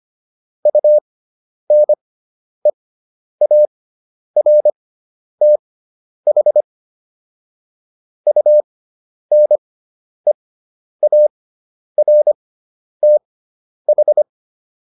Morse code, heard as a keyed tone, about 25 wpm, spaced as for 10 wpm, 600 Hz. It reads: UNEARTH UNEARTH